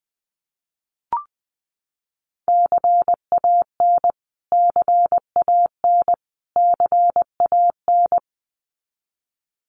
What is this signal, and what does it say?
Morse code, heard as a keyed tone, about 20 wpm, 700 Hz. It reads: CAN CAN CAN
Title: Top 200 Words Repeated 20wpm